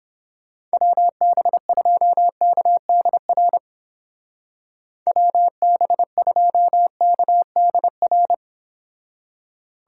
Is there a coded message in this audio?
Yes: Morse code from 0.7 to 8.3 s, heard as a keyed tone, about 29 wpm, 700 Hz. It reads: WB2KDR WB2KDR